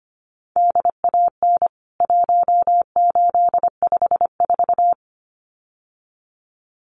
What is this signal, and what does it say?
Morse code, heard as a keyed tone, about 25 wpm, 700 Hz. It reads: DAN 1854
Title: CWT Exchange 25wpm